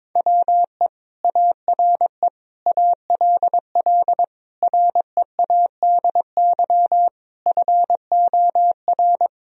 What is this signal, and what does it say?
Morse code, heard as a keyed tone, about 22 wpm, 700 Hz. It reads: WE ARE ALL READY FOR
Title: Sentences from Top 1000 Words - Review Part 4 - 22wpm